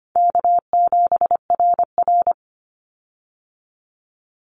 Morse code, heard as a keyed tone, about 25 wpm, 700 Hz. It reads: K7RR
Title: Easy Call Signs Repeated 3x - 25wpm